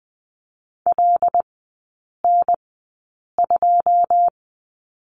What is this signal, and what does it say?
Morse code, heard as a keyed tone, about 20 wpm, spaced as for 8 wpm, 700 Hz. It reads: LN2
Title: 093 Lesson 16e - Letter c - Three Characters 20wpm